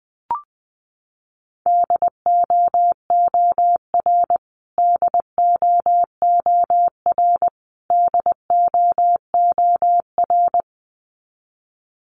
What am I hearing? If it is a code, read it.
Morse code, heard as a keyed tone, about 20 wpm, 700 Hz. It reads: DOOR DOOR DOOR
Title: Top 500 Words Repeated 20wpm